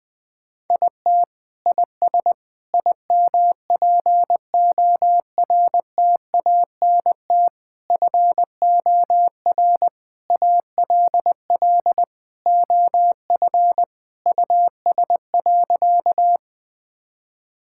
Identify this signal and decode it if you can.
Morse code, heard as a keyed tone, about 21 wpm, 690 Hz. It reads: IT IS IMPORTANT FOR ALL OF US.
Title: Sentences with 4-Word Phrases - Spoken Beginning - 20wpm